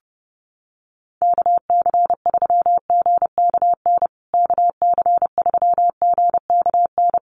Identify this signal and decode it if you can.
Morse code, heard as a keyed tone, about 30 wpm, 700 Hz. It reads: KC3GKN KC3GKN
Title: USA Call Signs Repeated 2x 30wpm